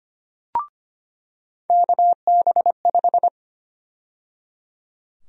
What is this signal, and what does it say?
Morse code, heard as a keyed tone, about 25 wpm, 700 Hz. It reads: KB5
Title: USA Call Sign Prefixes - 25wpm